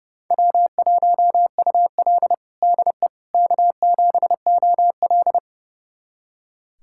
Morse code, heard as a keyed tone, about 30 wpm, 700 Hz. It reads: W1UL DE K7OL
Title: USA Callsign DE Callsign - 4 Characters - 30wpm